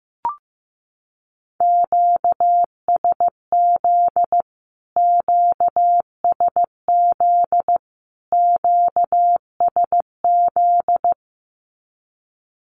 Morse code, heard as a keyed tone, about 15 wpm, 700 Hz. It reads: QSZ QSZ QSZ
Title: Q-Codes Repeated 3x 15wpm